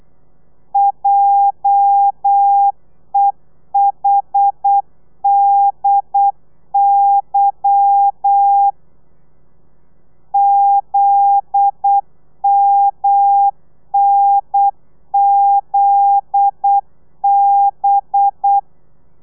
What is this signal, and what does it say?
Morse code, heard as a keyed tone, about 8 wpm, 800 Hz. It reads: JEHDY ZMNZB
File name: telegrafia / i2tfj / Lez11 35